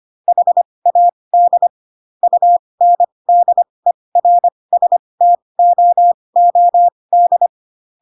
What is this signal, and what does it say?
Morse code, heard as a keyed tone, about 25 wpm, spaced as for 20 wpm, 700 Hz. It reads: HAD UNDERSTOOD
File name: Pinocchio / Pinocchio ch7 15 25 ik1hge